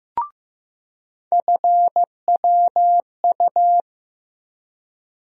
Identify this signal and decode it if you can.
Morse code, heard as a keyed tone, about 15 wpm, 700 Hz. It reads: FWU
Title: USA Call Sign Postfixes - 3 Letters 15wpm